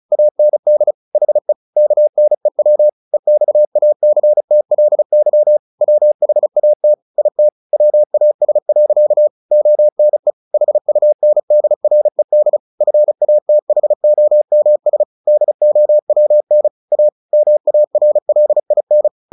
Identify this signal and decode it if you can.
Morse code, heard as a keyed tone, about 34 wpm, 600 Hz. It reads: AND HE KNEW EXACTLY WHAT IT WAS. ONE HUNDRED FATHOMS DOWN A MARLIN
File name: the old man and the sea / the old man and the sea 1 35wpm